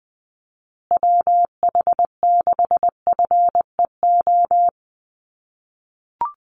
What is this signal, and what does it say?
Morse code, heard as a keyed tone, about 20 wpm, 700 Hz. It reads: WH6FEO
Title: USA Call Signs Repeated 3x 20wpm